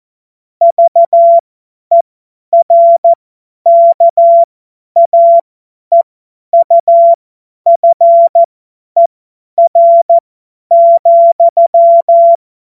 Morse code, heard as a keyed tone, about 14 wpm, spaced as for 10 wpm, 680 Hz. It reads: VERKAEUFER,
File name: Morsekurs / Witze / Witz 16 Januar 2024 50 F70